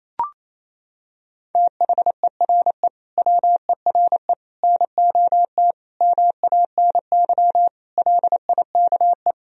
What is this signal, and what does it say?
Morse code, heard as a keyed tone, about 28 wpm, 700 Hz. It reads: THERE WERE NOT MANY LIKE
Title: Sentences from Top 1000 Words - Review Part 1 - 28wpm